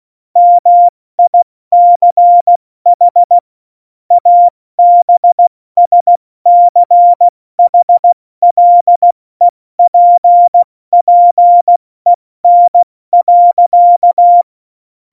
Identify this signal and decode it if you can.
Morse code, heard as a keyed tone, about 16 wpm, 700 Hz. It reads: MICH ABSCHLEPPEN.
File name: Morsekurs / Witze / Witz 05. Februar 2019, 70 F80